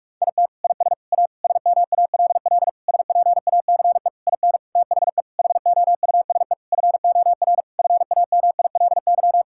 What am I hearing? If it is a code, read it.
Morse code, heard as a keyed tone, about 41 wpm, 700 Hz. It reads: IT IS A SMALL SPACE IN THE HOUSE FOR FAMILY
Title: Sentences from Top 1000 Words - Review Part 5 - 45wpm